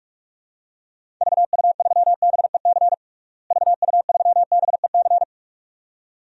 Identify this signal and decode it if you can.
Morse code, heard as a keyed tone, about 41 wpm, 700 Hz. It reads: VU3BEC VU3BEC
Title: DXCC-100 Call Signs - 6 Characters - Repeated 2x - 45wpm